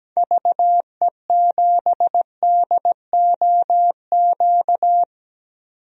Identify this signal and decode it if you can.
Morse code, heard as a keyed tone, about 17 wpm, 700 Hz. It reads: VE7DOQ
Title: DXCC-100 Call Signs - 6 Characters - 17wpm